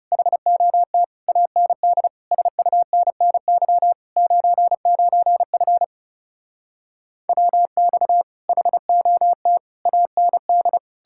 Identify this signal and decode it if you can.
Morse code, heard as a keyed tone, about 40 wpm, 700 Hz. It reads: HOT AND SUNNY 99F WX HOT AND